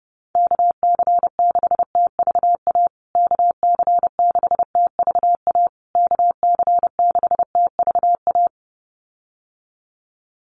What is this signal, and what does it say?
Morse code, heard as a keyed tone, about 30 wpm, 700 Hz. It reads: KC6TVA KC6TVA KC6TVA
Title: USA Call Signs Repeated 3x 30wpm